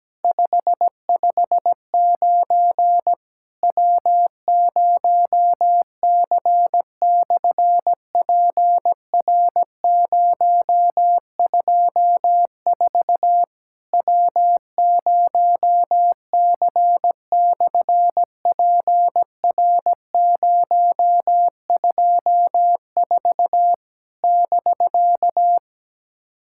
Morse code, heard as a keyed tone, about 17 wpm, 700 Hz. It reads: 559 W0C/PR024 W0C/PR024 <BK>
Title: SOTA QSO Practice 17wpm